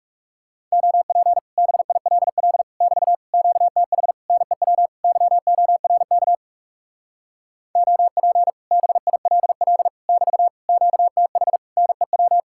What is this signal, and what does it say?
Morse code, heard as a keyed tone, about 40 wpm, 700 Hz. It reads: OP BILL <BT> QTH NEW YORK OP BILL <BT> QTH NEW